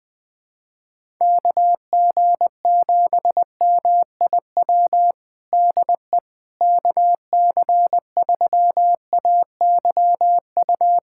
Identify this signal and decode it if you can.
Morse code, heard as a keyed tone, about 20 wpm, 700 Hz. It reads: KG7MIW DE KC3AYU
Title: USA Callsign DE Callsign 20wpm